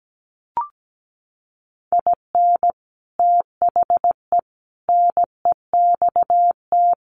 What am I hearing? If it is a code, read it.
Morse code, heard as a keyed tone, about 17 wpm, 700 Hz. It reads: IN THE NEXT
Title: Sets of 5 Words Encore2 17wpm